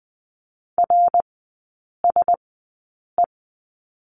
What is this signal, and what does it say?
Morse code, heard as a keyed tone, about 20 wpm, spaced as for 8 wpm, 700 Hz. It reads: RSE
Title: 053 Lesson 10e - Letter r - Three Characters 20wpm